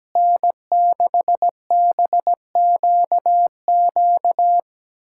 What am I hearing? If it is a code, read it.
Morse code, heard as a keyed tone, about 17 wpm, 700 Hz. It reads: N6BQQ
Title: K1USN SST QSO Practice - 17wpm